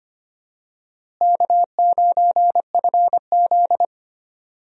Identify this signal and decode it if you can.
Morse code, heard as a keyed tone, about 25 wpm, 700 Hz. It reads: K9FZ
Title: USA Call Signs - 4 Characters - Repeated 2x - 25wpm